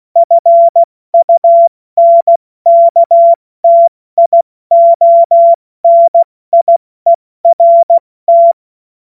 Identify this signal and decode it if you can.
Morse code, heard as a keyed tone, about 16 wpm, 680 Hz. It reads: FUNKTIONIERT
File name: Morsekurs / Witze / Witz 19. Oktober 2021, 70 F80